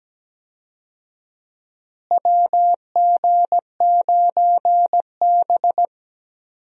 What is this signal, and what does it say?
Morse code, heard as a keyed tone, about 17 wpm, 700 Hz. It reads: WG9B
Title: USA Call Signs - 4 Characters - Repeated 2x - 17wpm